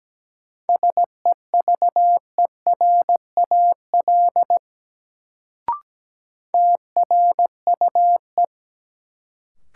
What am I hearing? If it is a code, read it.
Morse code, heard as a keyed tone, about 17 wpm, 700 Hz. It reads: SEVERAL TRUE
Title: Sentences from Top 400 Words - New Word Drill - 17wpm